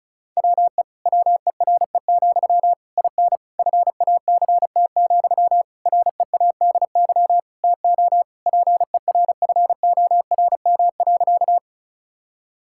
Morse code, heard as a keyed tone, about 36 wpm, 700 Hz. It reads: WE WERE, IN FACT, READY TO PERFORM.
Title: Sentences from Top 1000 Words - Review Part 5 - 35wpm